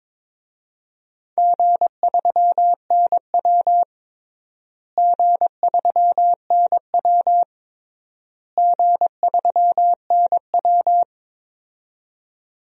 Morse code, heard as a keyed tone, about 22 wpm, 700 Hz. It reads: G3NW G3NW G3NW
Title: DXCC-100 Call Signs - 4 Characters - Repeated 3x - 22wpm